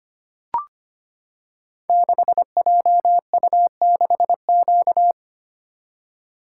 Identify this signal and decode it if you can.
Morse code, heard as a keyed tone, about 25 wpm, 700 Hz. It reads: 6JU6Q
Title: Code Groups - 5 Letter:Numbers - 25wpm